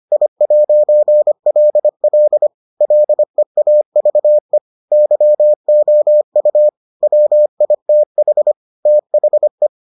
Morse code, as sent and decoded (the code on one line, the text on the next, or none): .. .----. .-.. .-.. / .-.. .- ...- . / -.-- --- ..- / .-- .. - .... / - .... .
I'LL LEAVE YOU WITH THE